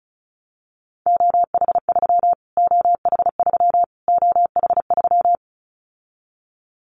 Morse code, heard as a keyed tone, about 34 wpm, 700 Hz. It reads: OH3 OH3 OH3
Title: Callsign Prefixes for Top 25 DXCC - Repeated 3x - 35wpm